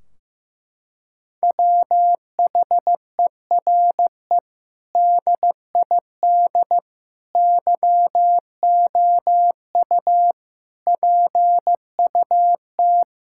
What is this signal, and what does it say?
Morse code, heard as a keyed tone, about 16 wpm, 700 Hz. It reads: WHERE DID YOU PUT
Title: Sentences from Top 1600 Words 15wpm